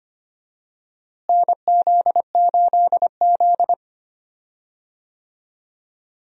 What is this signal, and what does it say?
Morse code, heard as a keyed tone, about 25 wpm, 700 Hz. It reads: NZ8Z
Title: USA Call Signs - 4 Characters - Repeated 2x - 25wpm